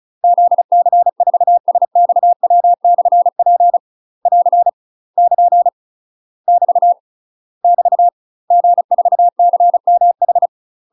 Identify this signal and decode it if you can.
Morse code, heard as a keyed tone, about 34 wpm, 700 Hz. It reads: ZC4SXW/P <AR> <KN> <BT> <BT> G4CMH
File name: UK MASTER QSOs TRAINING/35WPM